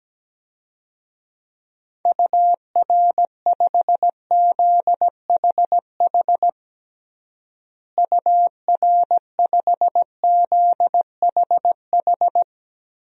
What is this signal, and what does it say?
Morse code, heard as a keyed tone, about 17 wpm, 700 Hz. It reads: UR5ZHH UR5ZHH
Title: DXCC-100 Call Signs - 6 Characters - Repeated 3x - 17wpm